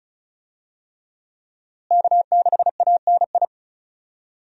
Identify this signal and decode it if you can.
Morse code, heard as a keyed tone, about 36 wpm, 700 Hz. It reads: K6ANI
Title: USA Call Signs - 5 Characters - Repeated 3x - 35wpm